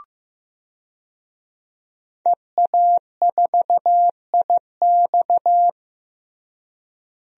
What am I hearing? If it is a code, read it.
Morse code, heard as a keyed tone, about 15 wpm, 700 Hz. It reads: EA4IX
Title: DXCC-100 Call Signs - 5 Characters - Repeated 3x - 15wpm